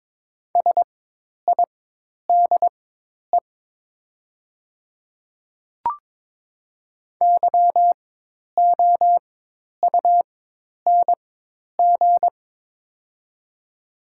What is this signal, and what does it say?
Morse code, heard as a keyed tone, about 22 wpm, spaced as for 10 wpm, 700 Hz. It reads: SIDE YOUNG
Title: 4x Character Spacing - Top 500 Words - 22wpm